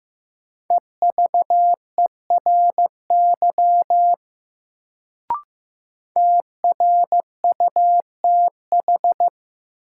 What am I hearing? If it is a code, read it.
Morse code, heard as a keyed tone, about 15 wpm, 700 Hz. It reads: EVERY TRUTH
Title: Sets of 5 Words - New Word Drill - 15wpm